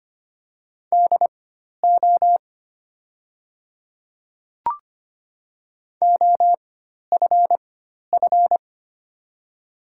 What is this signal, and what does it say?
Morse code, heard as a keyed tone, about 25 wpm, spaced as for 12 wpm, 700 Hz. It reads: DO OFF
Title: 4x Character Spacing - Top 200 Words - 25wpm